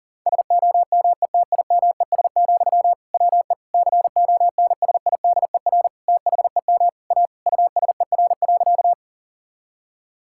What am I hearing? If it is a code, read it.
Morse code, heard as a keyed tone, about 40 wpm, 690 Hz. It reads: SOMETIMES, WE CONSIDER THEM A USER.
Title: Sentences from Top 1000 Words - Review Part 5 - 40wpm